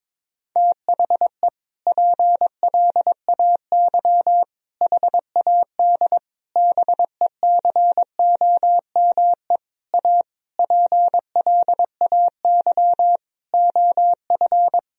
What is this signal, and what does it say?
Morse code, heard as a keyed tone, about 22 wpm, 700 Hz. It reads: THE PLAY HAD BECOME A PLAY OF
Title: Sentences from Top 1000 Words - Review Part 4 - 22wpm